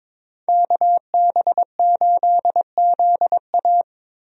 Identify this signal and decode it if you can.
Morse code, heard as a keyed tone, about 22 wpm, 700 Hz. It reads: KB8ZA
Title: USA Call Signs - 5 Characters - Repeated 3x - 22wpm